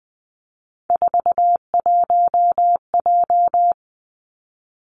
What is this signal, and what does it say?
Morse code, heard as a keyed tone, about 20 wpm, 700 Hz. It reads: 41J
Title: Three Challenging Letter-Number Combinations 20wpm